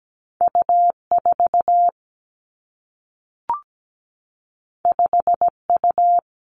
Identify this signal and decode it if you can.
Morse code, heard as a keyed tone, about 17 wpm, 700 Hz. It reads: U 4 5 U
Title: Two Challenging Letter-Number Combinations 17wpm